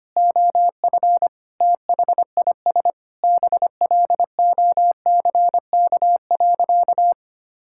Morse code, heard as a keyed tone, about 25 wpm, 700 Hz. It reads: OF THIS BLOCK.
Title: Sentences from Top 1600 Words 25wpm